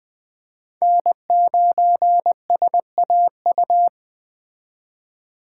Morse code, heard as a keyed tone, about 20 wpm, 700 Hz. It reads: N9SAU
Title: Speed Racing - USA Call Signs 17wpm